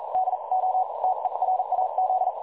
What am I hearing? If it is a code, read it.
Morse code, heard as a keyed tone, about 40 wpm, spaced as for 35 wpm, 750 Hz, with noise. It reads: DO TERAZ
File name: dlazaawansowanych / nagraniazpasma / websdr recording 3565.0kHz